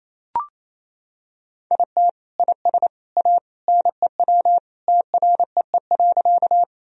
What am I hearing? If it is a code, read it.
Morse code, heard as a keyed tone, about 27 wpm, 700 Hz. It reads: IT IS A NEW TREE.
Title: Sentences from Top 1000 Words - Review Part 1 - 28wpm